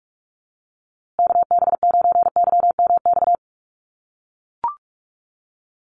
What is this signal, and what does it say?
Morse code, heard as a keyed tone, about 44 wpm, 700 Hz. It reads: KB9YMX
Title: USA Call Signs Repeated 3x 45wpm